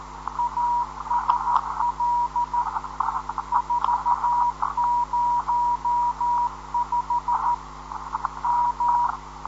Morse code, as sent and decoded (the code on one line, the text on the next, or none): .- ... .-.. / ....- ----- ..... / --
ASL 405 M